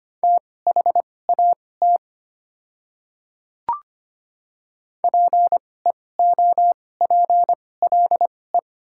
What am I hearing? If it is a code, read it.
Morse code, heard as a keyed tone, about 25 wpm, spaced as for 18 wpm, 700 Hz. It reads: THAT PEOPLE
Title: 2x Character Spacing - Top 200 Words - 25wpm